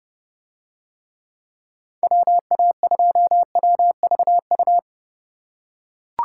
Morse code, heard as a keyed tone, about 28 wpm, 700 Hz. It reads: WA2WVU